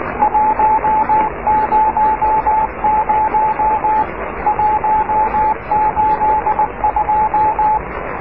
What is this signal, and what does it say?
Morse code, heard as a keyed tone, about 19 wpm, 870 Hz, with radio noise. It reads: 100 182
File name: oscar 7 de k5dze section 2